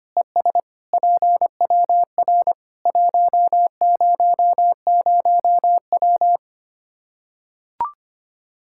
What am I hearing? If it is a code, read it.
Morse code, heard as a keyed tone, about 25 wpm, 700 Hz. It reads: ES PWR 100W
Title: K1USN SST QSO Practice 25wpm